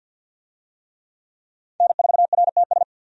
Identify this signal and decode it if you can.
Morse code, heard as a keyed tone, about 50 wpm, 700 Hz. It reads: N4RTS
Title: USA Call Signs - 5 Characters - 50wpm